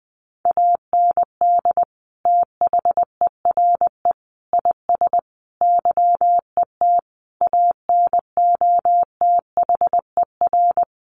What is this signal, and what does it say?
Morse code, heard as a keyed tone, about 20 wpm, 700 Hz. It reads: AND THERE IS YET ANOTHER
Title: Sentences from Top 1500 Words 20wpm